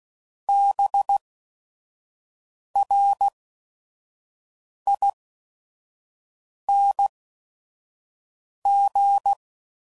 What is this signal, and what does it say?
Morse code, heard as a keyed tone, about 16 wpm, spaced as for 5 wpm, 800 Hz. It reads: BRING
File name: Suffixes / Special ING's / VerySlowToFaster ING's